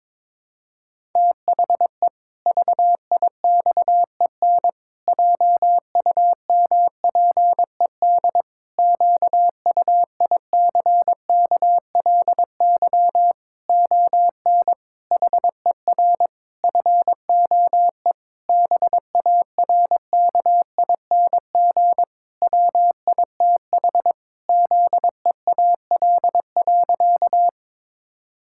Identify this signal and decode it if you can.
Morse code, heard as a keyed tone, about 22 wpm, 700 Hz. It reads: THE VIXEN JUMPED QUICKLY ON HER FOE BARKING WITH ZEAL.
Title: Pangram Challenge v2 - 22wpm